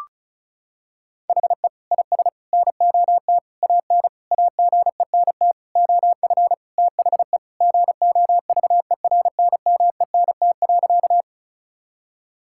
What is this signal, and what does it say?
Morse code, heard as a keyed tone, about 36 wpm, 700 Hz. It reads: HE IS NOT AN AGENT OF THE GOVERNMENT.
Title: Sentences from Top 1600 Words 35wpm